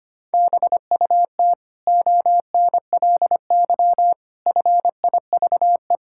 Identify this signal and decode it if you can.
Morse code, heard as a keyed tone, about 25 wpm, 700 Hz. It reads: BUT ONLY FIVE
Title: Sentences from Top 1000 Words - Review Part 6 - 25wpm